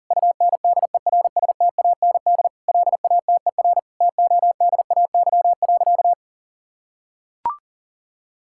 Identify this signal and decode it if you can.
Morse code, heard as a keyed tone, about 40 wpm, 700 Hz. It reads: UNDERSTAND LATER TODAY.
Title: Sentences from Top 1000 Words - Review Part 5 - 40wpm